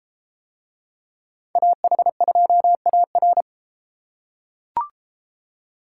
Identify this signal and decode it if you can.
Morse code, heard as a keyed tone, about 31 wpm, 700 Hz. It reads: AH2AR